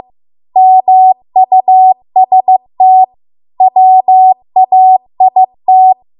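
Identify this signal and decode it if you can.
Morse code, heard as a keyed tone, about 15 wpm, 750 Hz. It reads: MUST WAIT
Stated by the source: Morse code practice at 15 wpm from the national amateur radio association's code archive.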